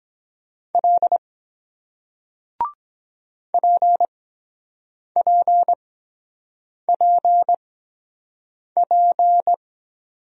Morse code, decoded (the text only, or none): L PPPP